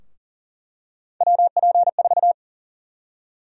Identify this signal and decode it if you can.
Morse code, heard as a keyed tone, about 40 wpm, 700 Hz. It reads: WP4
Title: USA Call Sign Prefixes - 40wpm